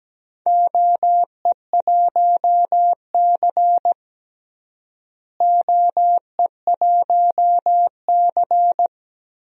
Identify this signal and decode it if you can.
Morse code, heard as a keyed tone, about 17 wpm, 700 Hz. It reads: OE1C OE1C
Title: DXCC-100 Call Signs - 4 Characters - Repeated 3x - 17wpm